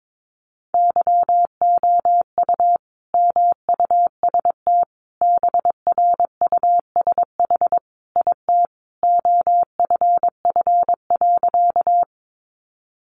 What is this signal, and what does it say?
Morse code, heard as a keyed tone, about 22 wpm, 700 Hz. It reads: YOU MUST BRUSH IT OFF.
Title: Sentences from Top 1100 Words 22wpm